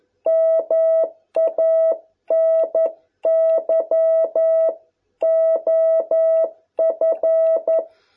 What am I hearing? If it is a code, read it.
Morse code, heard as a keyed tone, about 11 wpm, 660 Hz. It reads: MANY OF